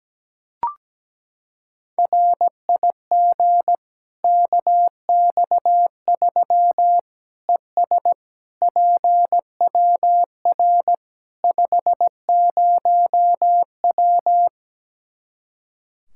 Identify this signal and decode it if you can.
Morse code, heard as a keyed tone, about 17 wpm, 700 Hz. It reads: RIG KX3 ES PWR 50W
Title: Practice QSO - Exchange 2 - Info - 17wpm